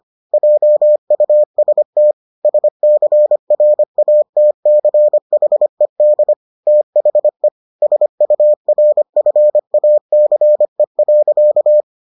Morse code, heard as a keyed tone, about 25 wpm, 600 Hz. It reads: JUST SCRATCHED THE SURFACE.